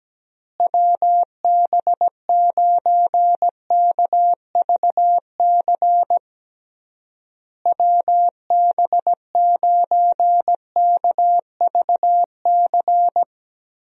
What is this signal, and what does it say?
Morse code, heard as a keyed tone, about 17 wpm, 700 Hz. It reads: WB9KVC WB9KVC
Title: USA Call Signs - 6 Characters - Repeated 3x - 17wpm